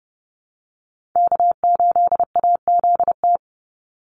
Morse code, heard as a keyed tone, about 30 wpm, 700 Hz. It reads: K8AZT
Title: Call Signs Repeated 2x - 30wpm